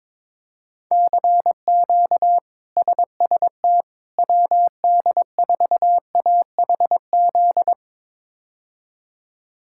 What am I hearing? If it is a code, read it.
Morse code, heard as a keyed tone, about 22 wpm, 700 Hz. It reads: CQ SST WD4AHZ
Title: K1USN SST QSO Practice - 22wpm